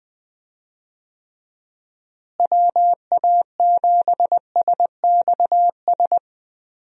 Morse code, heard as a keyed tone, about 20 wpm, 700 Hz. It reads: WA7SXS
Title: USA Call Signs - 6 Characters - Repeated 3x - 20wpm